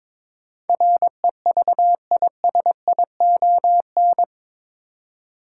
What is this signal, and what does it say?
Morse code, heard as a keyed tone, about 22 wpm, 700 Hz. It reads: REVISION
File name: practice / 22wpm / Eight Letter Words - 22wpm